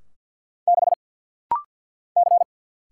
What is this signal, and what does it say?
Morse code, heard as a keyed tone, about 50 wpm, 700 Hz. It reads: 6 C
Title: Single Letter-Number - Rapid-Fire 50wpm